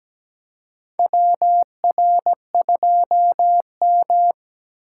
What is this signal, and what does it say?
Morse code, heard as a keyed tone, about 17 wpm, 700 Hz. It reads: WR2M
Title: USA Call Signs - 6x Character Spacing - 17wpm